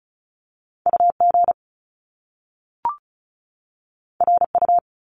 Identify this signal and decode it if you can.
Morse code, heard as a keyed tone, about 36 wpm, 700 Hz. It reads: U G R U
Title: Top Two Letter Combinations in Words 35wpm